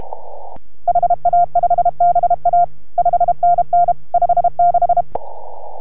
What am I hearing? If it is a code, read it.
Morse code, heard as a keyed tone, about 32 wpm, 690 Hz. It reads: HA5BA 5NN 56